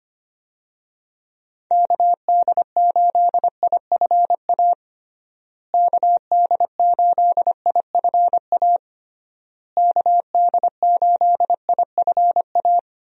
Morse code, heard as a keyed tone, about 25 wpm, 700 Hz. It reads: KD8IFA KD8IFA KD8IFA
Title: USA Call Signs - 6 Characters - Repeated 3x - 25wpm